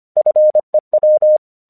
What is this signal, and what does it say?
Morse code, heard as a keyed tone, about 24 wpm, 610 Hz. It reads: FEW
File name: 1 English 25wpm / AQC11